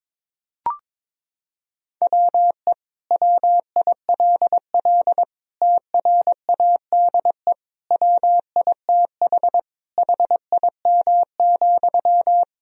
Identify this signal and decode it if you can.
Morse code, heard as a keyed tone, about 22 wpm, 700 Hz. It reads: WE WILL TRADE WITH HIM,
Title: Sentences from Top 1000 Words - Review Part 4 - 22wpm